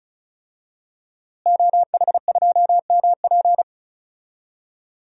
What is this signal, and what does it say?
Morse code, heard as a keyed tone, about 36 wpm, 700 Hz. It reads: OH2MP
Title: DXCC-100 Call Signs - 5 Characters - Repeated 2x - 35wpm